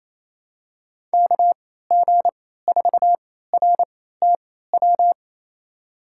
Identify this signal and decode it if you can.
Morse code, heard as a keyed tone, about 28 wpm, spaced as for 16 wpm, 700 Hz. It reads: KG4RTW
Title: USA Call Signs - 3x Character Spacing - 28wpm